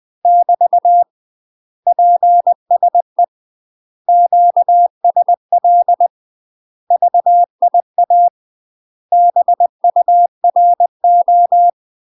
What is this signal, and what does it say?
Morse code, heard as a keyed tone, about 20 wpm, 700 Hz. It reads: <BT> PSE QSL VIA BURO
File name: UK MASTER QSOs TRAINING / 20WPM / UK MASTER QSOs 0008